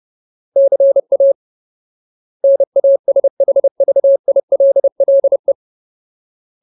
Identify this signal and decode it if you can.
Morse code, heard as a keyed tone, about 30 wpm, 550 Hz. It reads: CA NASHVILLE
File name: Ultimate Word List / 30WPM Ultimate